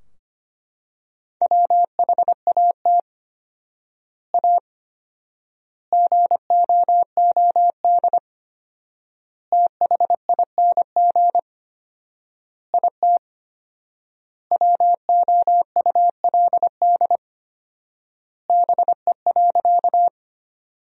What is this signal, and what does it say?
Morse code, heard as a keyed tone, about 25 wpm, 700 Hz. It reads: WHAT A GOOD THING IT WOULD BE.
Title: 4x Word-Spacing - Sentences from Top 200 Words - 25wpm